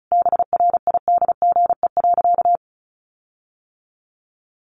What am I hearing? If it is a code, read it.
Morse code, heard as a keyed tone, about 34 wpm, 700 Hz. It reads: BRIDGE.
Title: Sentences from Top 1500 Words 35wpm